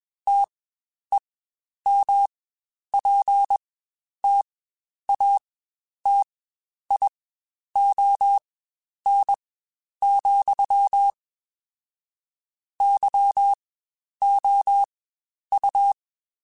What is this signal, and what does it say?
Morse code, heard as a keyed tone, about 21 wpm, spaced as for 10 wpm, 800 Hz. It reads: TEMPTATION, YOU
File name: Spaced / Chuckles Spaced-mp3s / Temptation sp @20